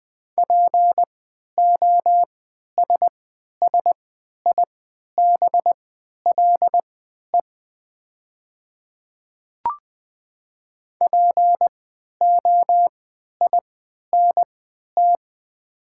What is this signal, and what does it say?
Morse code, heard as a keyed tone, about 21 wpm, spaced as for 11 wpm, 690 Hz. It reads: POSSIBLE POINT
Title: 3x Character Spacing - Top 500 Words - 20wpm